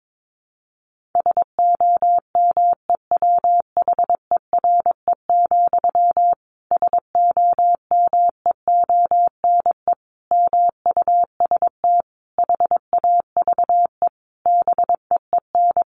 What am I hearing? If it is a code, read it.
Morse code, heard as a keyed tone, about 22 wpm, 700 Hz. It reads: SOMEWHERE, SOMEONE MUST HAVE BEEN
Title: A Sound Of Thunder 22wpm